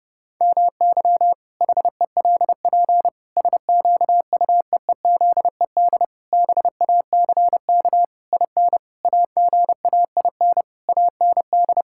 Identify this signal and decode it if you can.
Morse code, heard as a keyed tone, about 30 wpm, 700 Hz. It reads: MY HELP SQUEEZED BACK IN AGAIN AND
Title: Pangram Challenge v2 - 30wpm